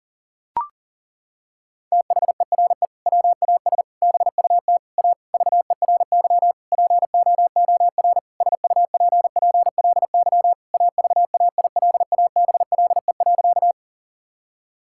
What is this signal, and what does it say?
Morse code, heard as a keyed tone, about 40 wpm, 700 Hz. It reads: THERE WAS BUT A VERY POOR SUPPLY AVAILABLE.
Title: Sentences from Top 1000 Words - Review Part 5 - 40wpm